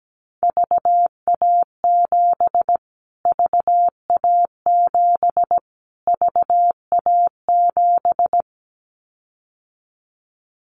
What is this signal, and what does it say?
Morse code, heard as a keyed tone, about 17 wpm, 700 Hz. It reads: VA7 VA7 VA7
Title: Callsign Prefixes for Top 25 DXCC - Repeated 3x - 17wpm